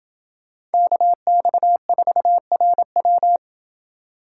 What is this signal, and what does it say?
Morse code, heard as a keyed tone, about 28 wpm, 700 Hz. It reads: KX4RW